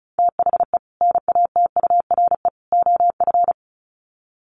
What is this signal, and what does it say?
Morse code, heard as a keyed tone, about 34 wpm, 700 Hz. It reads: THE NATURE OF
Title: Sets of 3 Words 35wpm